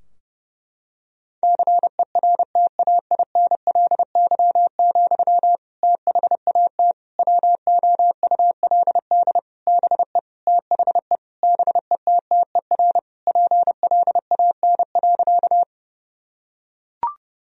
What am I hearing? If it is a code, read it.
Morse code, heard as a keyed tone, about 30 wpm, 700 Hz. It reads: CERTAINLY, THAT WOULD BE THE BETTER PLAN.
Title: Sentences from Top 1000 Words - Review Part 2 - 30wpm